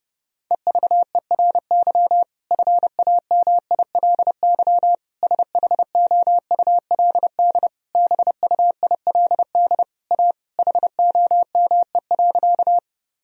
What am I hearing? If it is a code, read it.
Morse code, heard as a keyed tone, about 31 wpm, 700 Hz. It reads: EVERY FAMILY SHOULD BUILD A HOME.
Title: Sentences from Top 1000 Words - Review Part 1 - 30wpm